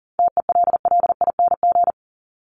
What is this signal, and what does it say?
Morse code, heard as a keyed tone, about 37 wpm, 700 Hz. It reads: TELLING
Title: ING Suffix - Top 100 Words with Base Words 40wpm